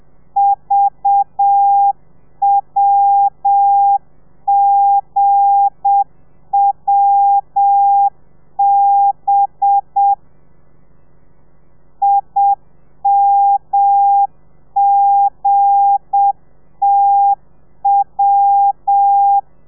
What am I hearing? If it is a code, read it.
Morse code, heard as a keyed tone, about 7 wpm, 800 Hz. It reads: VWGWB IMGTW
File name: telegrafia / i2tfj / Lez8 30